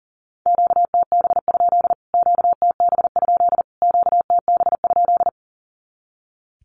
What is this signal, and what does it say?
Morse code, heard as a keyed tone, about 40 wpm, 700 Hz. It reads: QTB? QTB? QTB?
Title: Q-Codes Repeated 3x 40wpm